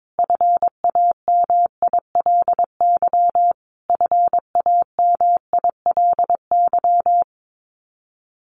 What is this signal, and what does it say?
Morse code, heard as a keyed tone, about 22 wpm, 700 Hz. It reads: FAMILY FAMILY
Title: Top 500 Words Repeated 22wpm